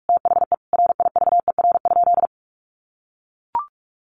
Morse code, heard as a keyed tone, about 44 wpm, 700 Hz. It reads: THE RIVER?
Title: Sentences from Top 1400 Words 45wpm